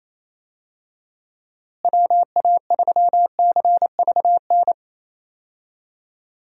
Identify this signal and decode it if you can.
Morse code, heard as a keyed tone, about 27 wpm, 700 Hz. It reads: WA3CVN
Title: USA Call Signs - 6 Characters - Repeated 3x - 28wpm